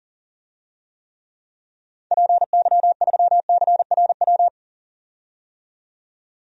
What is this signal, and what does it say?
Morse code, heard as a keyed tone, about 40 wpm, 700 Hz. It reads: PY3CRW
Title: DXCC-100 Call Signs - 6 Characters - 40wpm